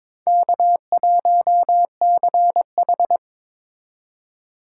Morse code, heard as a keyed tone, about 22 wpm, 700 Hz. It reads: K1CH
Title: USA Call Signs - 4 Characters - 2x Character Spacing - 22wpm